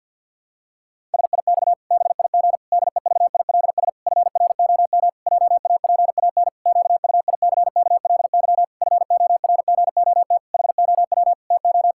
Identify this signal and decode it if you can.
Morse code, heard as a keyed tone, about 52 wpm, 700 Hz. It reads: SIX BIG DEVILS FROM JAPAN QUICKLY FORGOT HOW TO
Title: Pangram Challenge v2 - 50wpm